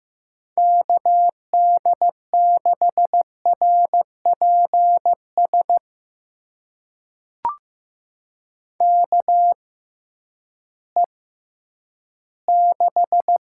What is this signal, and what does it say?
Morse code, heard as a keyed tone, about 15 wpm, spaced as for 10 wpm, 700 Hz. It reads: KD6RPS KE6
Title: USA Call Signs - 6 Characters - 6x Character Spacing - 15wpm